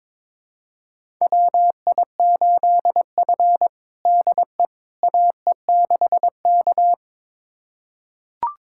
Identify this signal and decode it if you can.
Morse code, heard as a keyed tone, about 22 wpm, 700 Hz. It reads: WI8F DE AE6K
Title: USA Callsign DE Callsign - 4 Characters - 22wpm